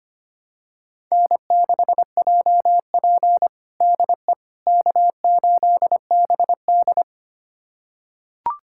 Morse code, heard as a keyed tone, about 25 wpm, 700 Hz. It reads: N6JP DE K8BD
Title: USA Callsign DE Callsign - 4 Characters - 25wpm